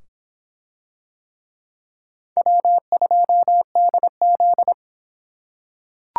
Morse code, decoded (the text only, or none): W2DZ